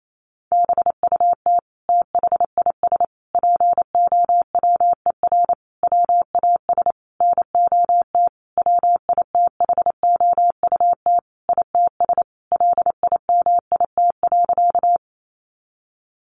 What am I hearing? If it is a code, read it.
Morse code, heard as a keyed tone, about 28 wpm, 700 Hz. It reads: BUT THIS POWER WAS NOT WITHOUT ITS LIMIT.
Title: Sentences from Top 1500 Words 28wpm